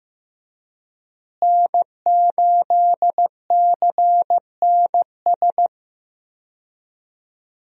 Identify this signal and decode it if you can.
Morse code, heard as a keyed tone, about 15 wpm, 700 Hz. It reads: N8CNS
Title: USA Call Signs - 5 Characters - Repeated 3x - 15wpm